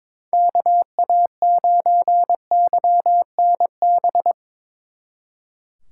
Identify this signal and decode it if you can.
Morse code, heard as a keyed tone, about 24 wpm, 700 Hz. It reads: KA9YNB
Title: Speed Racing - USA 2x3 Call Signs 22wpm